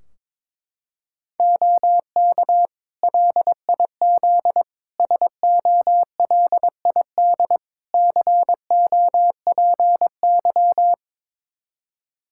Morse code, heard as a keyed tone, about 22 wpm, 700 Hz. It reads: OK LIZ SOLID COPY
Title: K1USN SST QSO Practice 22wpm